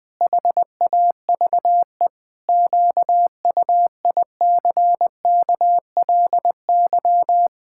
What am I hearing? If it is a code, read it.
Morse code, heard as a keyed tone, about 20 wpm, 700 Hz. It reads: HAVE QUICKLY
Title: Pangram Challenge v2 - 20wpm